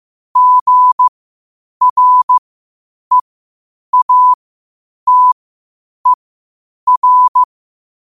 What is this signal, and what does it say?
Morse code, heard as a keyed tone, about 15 wpm, spaced as for 8 wpm, 1000 Hz. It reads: GREATER
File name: qotdfiles / 8wpm